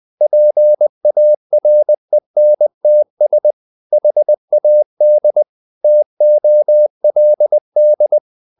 Morse code, heard as a keyed tone, about 20 wpm, 600 Hz. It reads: PARENTS HAD TOLD
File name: the old man and the sea / the old man and the sea part 1 20wpm